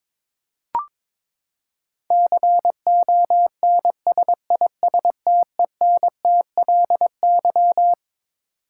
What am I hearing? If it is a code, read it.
Morse code, heard as a keyed tone, about 22 wpm, 700 Hz. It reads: CONSISTENTLY
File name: Twelve Letter Words - 22wpm